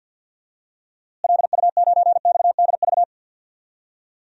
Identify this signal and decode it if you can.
Morse code, heard as a keyed tone, about 51 wpm, 700 Hz. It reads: LU9XDV
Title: DXCC-100 Call Signs - 6 Characters - Repeated 3x - 50wpm